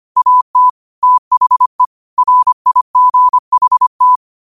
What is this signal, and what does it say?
Morse code, heard as a keyed tone, about 25 wpm, 1000 Hz. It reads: AT THE RIGHT
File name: qotdfiles / 25wpm